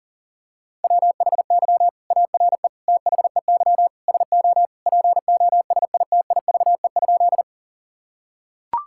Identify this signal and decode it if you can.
Morse code, heard as a keyed tone, about 40 wpm, 700 Hz. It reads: WHY ARE THEY SO POSITIVE?
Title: Sentences from Top 1000 Words - Review Part 5 - 40wpm